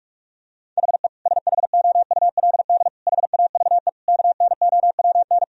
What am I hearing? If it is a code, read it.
Morse code, heard as a keyed tone, about 46 wpm, 700 Hz. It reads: HE SHOULD HAVE KNOWN
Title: Sentences from Top 1000 Words - Review Part 3 - 45wpm